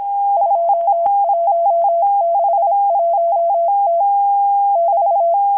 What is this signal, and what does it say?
Morse code, heard as a keyed tone, about 26 wpm, 680 Hz. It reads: 2161T <BT>
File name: HM02- -0437utc-7351khz